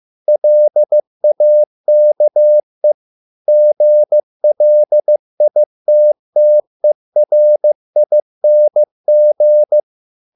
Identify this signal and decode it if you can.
Morse code, heard as a keyed tone, about 15 wpm, 600 Hz. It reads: LAKE GLITTERING